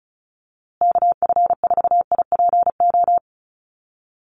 Morse code, heard as a keyed tone, about 34 wpm, 700 Hz. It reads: KF4IPO
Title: USA Call Signs Repeated 2x 35wpm